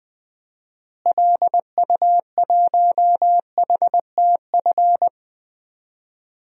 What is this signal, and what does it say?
Morse code, heard as a keyed tone, about 21 wpm, 700 Hz. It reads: LU1HTF
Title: DXCC-100 Call Signs - 6 Characters - Repeated 2x - 20wpm